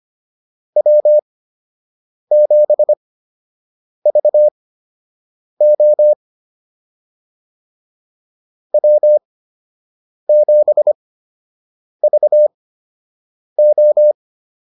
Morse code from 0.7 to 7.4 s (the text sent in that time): W7VO